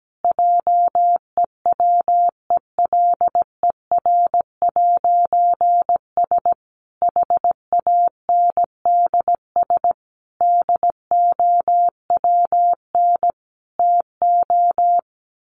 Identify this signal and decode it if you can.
Morse code, heard as a keyed tone, about 17 wpm, 700 Hz. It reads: JEWELER'S HANDS DOWN TO
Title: A Sound Of Thunder 17wpm